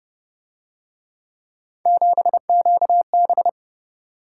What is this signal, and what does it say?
Morse code, heard as a keyed tone, about 32 wpm, 700 Hz. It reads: 7QB